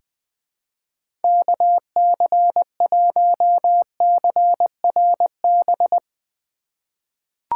Morse code, heard as a keyed tone, about 20 wpm, 700 Hz. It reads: KC1CRB
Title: USA Call Signs - 6 Characters - 3x Character Spacing - 20wpm